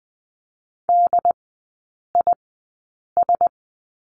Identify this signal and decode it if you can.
Morse code, heard as a keyed tone, about 20 wpm, spaced as for 8 wpm, 700 Hz. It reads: DIS
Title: 074 Lesson 13e - Letter L - Three Characters 20wpm